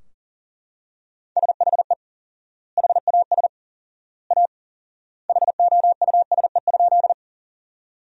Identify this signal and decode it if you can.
Morse code, heard as a keyed tone, about 40 wpm, 700 Hz. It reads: SHE HAS A HOUSE?
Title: 4x Word-Spacing - Sentences from Top 200 Words - 40wpm